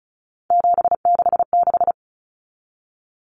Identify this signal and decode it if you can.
Morse code, heard as a keyed tone, about 34 wpm, 700 Hz. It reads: 766
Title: Common Cut Numbers - 3-Digit 35wpm